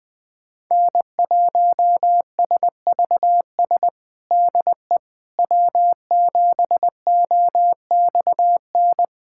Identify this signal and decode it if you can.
Morse code, heard as a keyed tone, about 20 wpm, 700 Hz. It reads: N1SVS DE W7OXN
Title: USA Callsign DE Callsign - 5 Characters - 20wpm